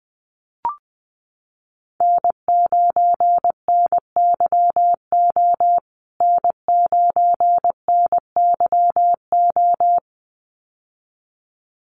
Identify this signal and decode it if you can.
Morse code, heard as a keyed tone, about 20 wpm, 700 Hz. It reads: N9NYO N9NYO
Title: USA Call Signs Repeated 2x 20wpm